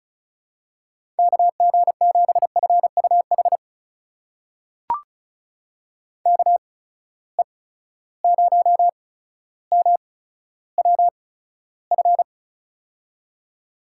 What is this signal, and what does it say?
Morse code, heard as a keyed tone, about 36 wpm, spaced as for 15 wpm, 700 Hz. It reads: KG7FUH KE0MWF
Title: USA Call Signs - 6 Characters - 8x Character Spacing - 35wpm